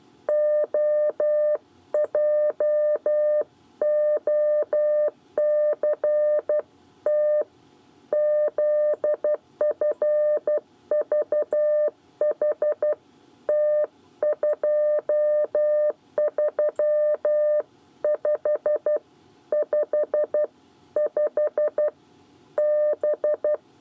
Morse code, heard as a keyed tone, about 12 wpm, 600 Hz. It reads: OJOCT ZFVHT 23555 B
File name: 8 bis lezione cw 1